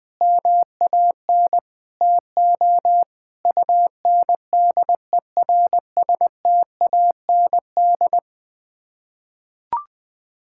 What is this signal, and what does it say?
Morse code, heard as a keyed tone, about 20 wpm, 700 Hz. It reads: MAN TO UNDERSTAND
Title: Sets of 9 Words - 20wpm